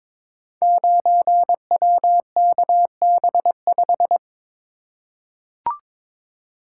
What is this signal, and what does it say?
Morse code, heard as a keyed tone, about 22 wpm, 700 Hz. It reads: 9WKB5
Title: Code Groups - 5 Letter:Numbers - 22wpm